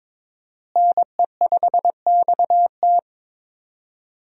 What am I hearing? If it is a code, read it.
Morse code, heard as a keyed tone, about 22 wpm, 700 Hz. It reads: NE5XT
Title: USA Call Signs - 5 Characters - Repeated 3x - 22wpm